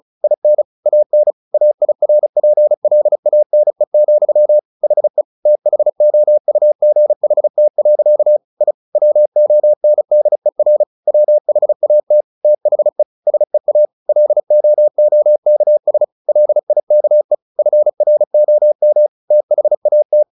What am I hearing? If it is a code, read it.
Morse code, heard as a keyed tone, about 34 wpm, 600 Hz. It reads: IN AN AIRPLANE, HE THOUGHT. I WONDER WHAT THE SEA LOOKS LIKE FROM THAT
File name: the old man and the sea 2 35wpm